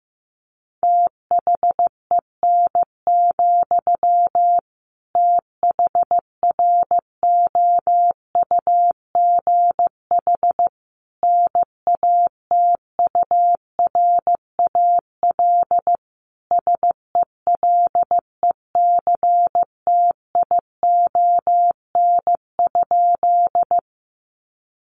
Morse code, heard as a keyed tone, about 15 wpm, 700 Hz. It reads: THEN, THROUGH NATURAL SELECTION?
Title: Sentences from Top 1400 Words 15wpm